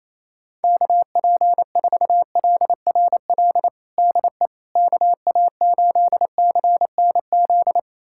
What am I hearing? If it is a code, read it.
Morse code, heard as a keyed tone, about 27 wpm, 700 Hz. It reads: KP4LRL DE KA8CNZ
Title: USA Callsign DE Callsign 28wpm